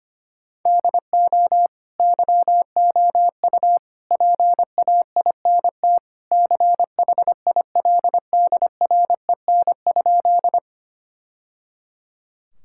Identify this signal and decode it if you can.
Morse code, heard as a keyed tone, about 25 wpm, 700 Hz. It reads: DO YOU PAINT CHILDREN?
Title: Sentences from Top 1000 Words - Review Part 6 - 25wpm